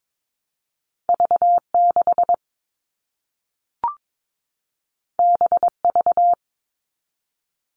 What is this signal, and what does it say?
Morse code, heard as a keyed tone, about 22 wpm, 700 Hz. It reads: V6 BV
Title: All Cut Numbers - 2-Digit 22wpm